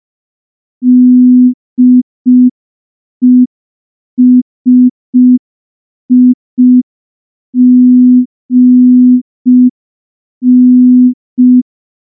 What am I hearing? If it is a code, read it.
Morse code, heard as a keyed tone, about 5 wpm, 250 Hz. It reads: DESIGN
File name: morse / morse-training-05WPM